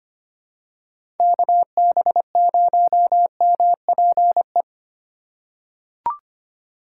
Morse code, heard as a keyed tone, about 25 wpm, 700 Hz. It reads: KB0MPE
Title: USA Call Signs - 6 Characters - 3x Character Spacing - 25wpm